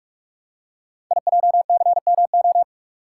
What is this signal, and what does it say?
Morse code, heard as a keyed tone, about 44 wpm, 700 Hz. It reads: IJCMO
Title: Code Groups - 5 Letters - 45wpm